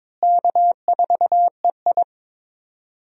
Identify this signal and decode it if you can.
Morse code, heard as a keyed tone, about 21 wpm, 700 Hz. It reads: K4EI
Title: USA Call Signs - 4 Characters - Repeated 3x - 22wpm